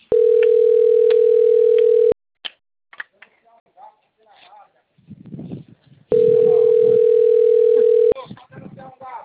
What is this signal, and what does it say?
Ringback tone (North America), 440 + 480 Hz, with speech.
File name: out- - - - - .11351